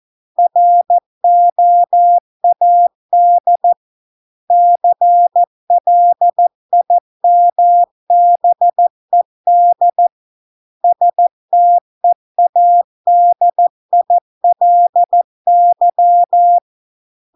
Morse code, heard as a keyed tone, about 14 wpm, 700 Hz. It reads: ROAD CLIMBED STEADILY